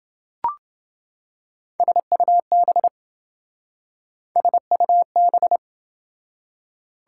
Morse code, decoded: SUB SUB

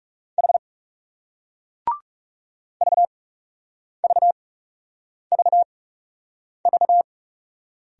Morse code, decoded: H VVVV

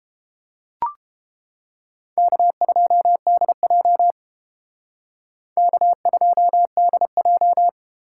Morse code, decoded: K2DJ K2DJ

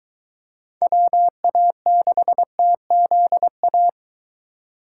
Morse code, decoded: WA6TZA

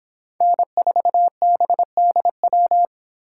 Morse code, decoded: N4BDW

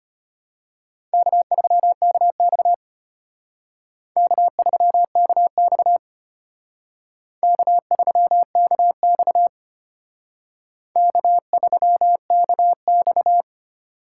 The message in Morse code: K3KX K3KX K3KX K3KX

